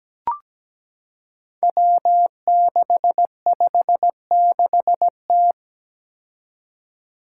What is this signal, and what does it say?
Morse code, heard as a keyed tone, about 17 wpm, 700 Hz. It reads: W656T
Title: Five Challenging Letter-Number Combinations 17wpm